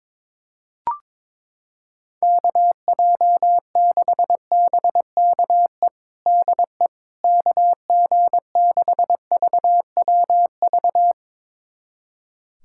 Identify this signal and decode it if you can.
Morse code, heard as a keyed tone, about 22 wpm, 700 Hz. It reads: KJ6BKE DE KG6VWV
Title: USA Callsign DE Callsign - 6 Characters - 22wpm